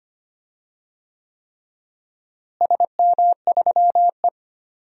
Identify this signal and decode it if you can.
Morse code, heard as a keyed tone, about 25 wpm, 700 Hz. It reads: SM3E
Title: DXCC-100 Call Signs - 4 Characters - Repeated 3x - 25wpm